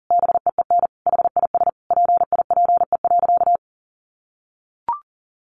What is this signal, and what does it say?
Morse code, heard as a keyed tone, about 40 wpm, 700 Hz. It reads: BEEN HIS PIPE.
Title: Sentences from Top 1500 Words 40wpm